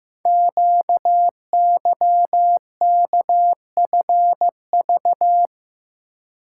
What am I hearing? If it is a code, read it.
Morse code, heard as a keyed tone, about 15 wpm, 700 Hz. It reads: QYKFV
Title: Code Groups - 5 Letters - 15wpm